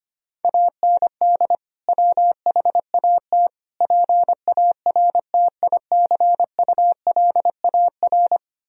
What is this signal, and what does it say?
Morse code, heard as a keyed tone, about 25 wpm, 700 Hz. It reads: AND WHAT PARTICULAR
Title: Sentences from Top 1000 Words - Review Part 4 - 25wpm